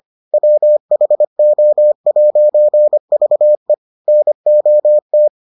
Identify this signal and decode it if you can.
Morse code, heard as a keyed tone, about 25 wpm, 600 Hz. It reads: WHO'VE NOT